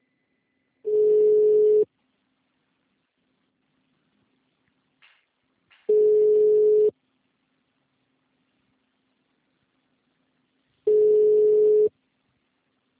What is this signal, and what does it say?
Ringback tone (Europe, 425 Hz).
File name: out- -2024- - - .25